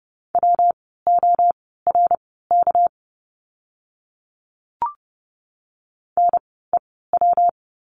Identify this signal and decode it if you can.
Morse code, heard as a keyed tone, about 30 wpm, spaced as for 17 wpm, 700 Hz. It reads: WORK NEW
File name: course / 30:10wpm / 230 Lesson 37f - Letter x - Words 30wpm